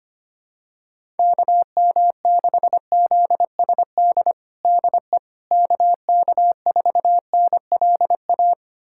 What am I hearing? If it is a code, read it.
Morse code, heard as a keyed tone, about 25 wpm, 700 Hz. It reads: KM6ZSD DE KK4NLA
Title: USA Callsign DE Callsign - 6 Characters - 25wpm